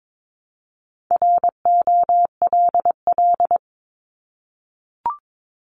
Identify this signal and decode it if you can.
Morse code, heard as a keyed tone, about 22 wpm, 700 Hz. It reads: ROLL
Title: Multi-letter Phonemes 22wpm